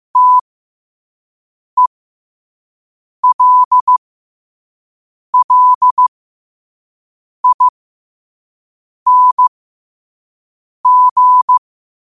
Morse code, heard as a keyed tone, about 15 wpm, spaced as for 5 wpm, 1000 Hz. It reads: TELLING
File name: qotdfiles / 5wpm